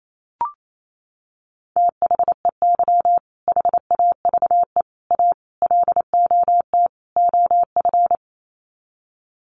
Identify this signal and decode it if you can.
Morse code, heard as a keyed tone, about 28 wpm, 700 Hz. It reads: THEY HAVE A LOT OF
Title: Sets of 5 Words Encore2 28wpm